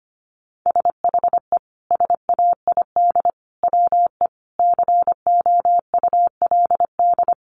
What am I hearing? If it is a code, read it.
Morse code, heard as a keyed tone, about 25 wpm, 700 Hz. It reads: SHE SAID WE COULD
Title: Sentences from Top 1300 Words 25wpm